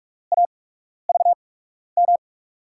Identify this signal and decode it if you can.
Morse code, heard as a keyed tone, about 46 wpm, 700 Hz. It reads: A V M
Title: USA Call Signs - 8x Character Spacing - 45wpm